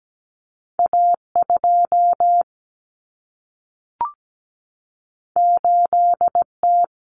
Morse code, heard as a keyed tone, about 17 wpm, 700 Hz. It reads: A2 8T
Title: Common Cut Numbers - 2-Digit 17wpm